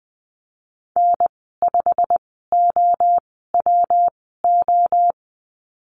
Morse code, heard as a keyed tone, about 20 wpm, spaced as for 15 wpm, 700 Hz. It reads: N5OWO
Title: 128 Lesson 21g - Number 6 - Call Signs 20wpm